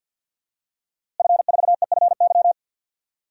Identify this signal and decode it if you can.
Morse code, heard as a keyed tone, about 49 wpm, 700 Hz. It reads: F4EFY
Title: DXCC-100 Call Signs - 5 Characters - Repeated 3x - 50wpm